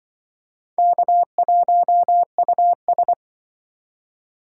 Morse code, heard as a keyed tone, about 23 wpm, 700 Hz. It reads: K1US
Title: Speed Racing - USA 1x2 Call Signs 20wpm